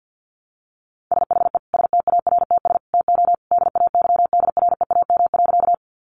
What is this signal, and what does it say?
Morse code, heard as a keyed tone, about 50 wpm, 700 Hz. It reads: SHE STARTS TO DAYDREAM.
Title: Easy Phrases 50wpm